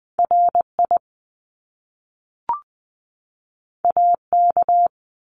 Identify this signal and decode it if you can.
Morse code, heard as a keyed tone, about 20 wpm, 700 Hz. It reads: RI AK
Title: US States 20wpm